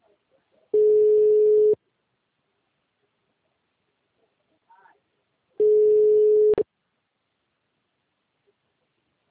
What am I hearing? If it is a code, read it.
Ringback tone (Europe, 425 Hz), with speech.
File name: out- -2005- - - .78437